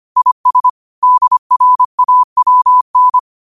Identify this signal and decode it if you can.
Morse code, heard as a keyed tone, about 25 wpm, 1000 Hz. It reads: IS DRAWN